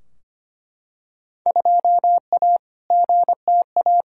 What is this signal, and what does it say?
Morse code, heard as a keyed tone, about 25 wpm, 700 Hz. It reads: 2A GTA
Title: Field Day Exchanges Encore - 25wpm